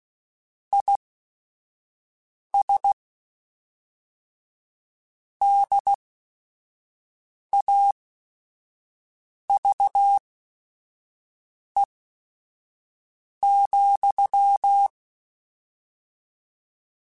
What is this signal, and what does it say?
Morse code, heard as a keyed tone, about 16 wpm, spaced as for 5 wpm, 800 Hz. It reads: IS DAVE,
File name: QSOs / QSOs-mp3s / WD0BGP 1@5